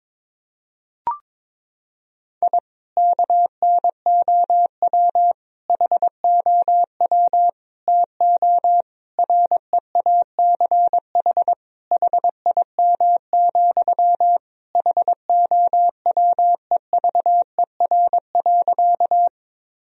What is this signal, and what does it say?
Morse code, heard as a keyed tone, about 22 wpm, 700 Hz. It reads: I KNOW HOW TO REACH HIM, HOWEVER.
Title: Sentences from Top 1000 Words - Review Part 2 - 22wpm